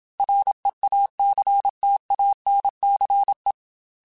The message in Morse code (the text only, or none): REACTANCE